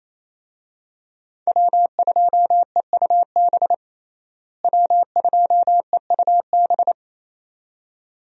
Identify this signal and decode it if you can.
Morse code, heard as a keyed tone, about 28 wpm, 700 Hz. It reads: W2EUB W2EUB
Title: USA Call Signs - 5 Characters - Repeated 2x - 28wpm